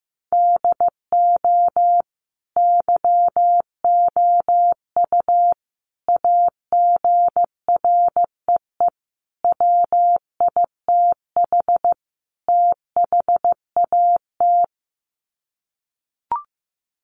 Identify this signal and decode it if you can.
Morse code, heard as a keyed tone, about 15 wpm, 700 Hz. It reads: DO YOU AGREE WITH THAT
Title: Sets of 5 Words Encore 15wpm